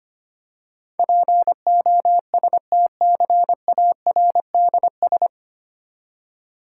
Morse code, heard as a keyed tone, about 25 wpm, 700 Hz. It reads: POSTCARDS
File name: practice / 25wpm / Difficult Words with Hints - 25wpm